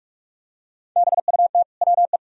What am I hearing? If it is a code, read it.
Morse code, heard as a keyed tone, about 46 wpm, 700 Hz. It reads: BUT WE
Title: Sentences from Top 1000 Words - Review Part 1 - 45wpm